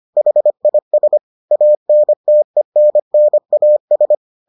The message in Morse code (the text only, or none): HIS ANTENNAS